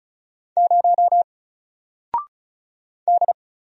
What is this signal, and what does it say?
Morse code, heard as a keyed tone, about 34 wpm, 700 Hz. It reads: 0 D
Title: Single Letter-Number 35wpm